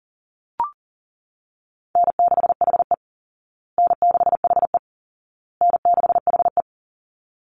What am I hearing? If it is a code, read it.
Morse code, heard as a keyed tone, about 40 wpm, 700 Hz. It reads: N6HE N6HE N6HE
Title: Easy Call Signs Repeated 3x - 40wpm